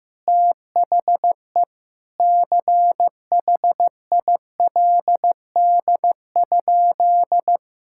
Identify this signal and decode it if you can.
Morse code, heard as a keyed tone, about 14 wpm, 700 Hz. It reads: THE CHILD?
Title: Sentences from Top 1600 Words 15wpm